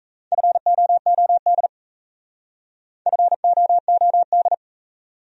Morse code, heard as a keyed tone, about 38 wpm, 700 Hz. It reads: FOOD FOOD